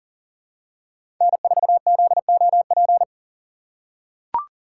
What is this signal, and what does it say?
Morse code, heard as a keyed tone, about 40 wpm, 700 Hz. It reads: N4ZOP
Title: USA Call Signs - 2x Character Spacing - 40wpm